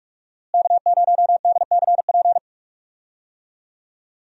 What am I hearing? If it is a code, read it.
Morse code, heard as a keyed tone, about 46 wpm, 700 Hz. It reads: K0DCP
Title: USA Call Signs - 5 Characters - Repeated 2x - 45wpm